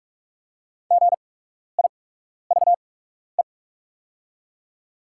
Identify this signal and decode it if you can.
Morse code, heard as a keyed tone, about 43 wpm, spaced as for 12 wpm, 700 Hz. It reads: GIVE
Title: 8x Character Spacing - Top 200 Words - 45wpm